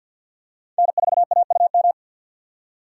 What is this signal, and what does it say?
Morse code, heard as a keyed tone, about 53 wpm, 690 Hz. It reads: N4AUM
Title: USA Call Signs - 5 Characters - 2x Character Spacing - 50wpm